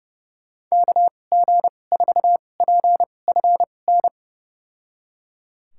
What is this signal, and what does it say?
Morse code, heard as a keyed tone, about 30 wpm, spaced as for 22 wpm, 700 Hz. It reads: KG4PFN
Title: USA Call Signs - 2x Character Spacing - 30wpm